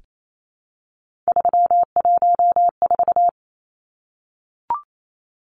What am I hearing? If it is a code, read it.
Morse code, heard as a keyed tone, about 28 wpm, 700 Hz. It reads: 314
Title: Numbers - 3-Digit 28wpm